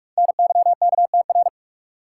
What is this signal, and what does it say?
Morse code, heard as a keyed tone, about 45 wpm, 700 Hz. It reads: NYKTR
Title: Code Groups - 5 Letters - 45wpm